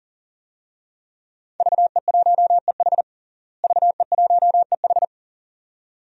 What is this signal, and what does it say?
Morse code, heard as a keyed tone, about 40 wpm, 700 Hz. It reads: VE1EH VE1EH
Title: DXCC-100 Call Signs - 5 Characters - Repeated 2x - 40wpm